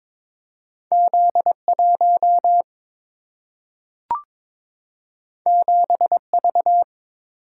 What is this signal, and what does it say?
Morse code, heard as a keyed tone, about 22 wpm, 700 Hz. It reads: Z 1 7 V
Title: Single-Letter Single-Number 22wpm